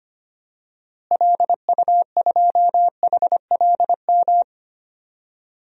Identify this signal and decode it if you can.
Morse code, heard as a keyed tone, about 25 wpm, 700 Hz. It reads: LU2HLM
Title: DXCC-100 Call Signs - 6 Characters - Repeated 3x - 25wpm